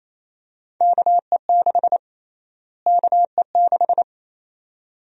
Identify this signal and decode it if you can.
Morse code, heard as a keyed tone, about 28 wpm, 700 Hz. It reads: KE6 KE6
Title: USA Call Sign Prefixes - Repeated 2x - 28wpm